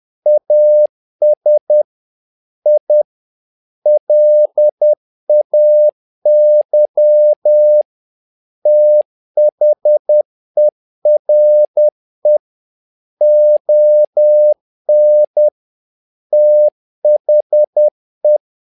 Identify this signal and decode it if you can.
Morse code, heard as a keyed tone, about 10 wpm, 600 Hz. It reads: AS I LAY THERE ON THE
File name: learning center / tower trouble / tower trouble 10wpm